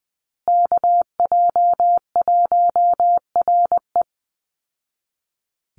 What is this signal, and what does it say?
Morse code, heard as a keyed tone, about 20 wpm, 700 Hz. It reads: KJ1RE
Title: Call Signs Repeated 3x - 20wpm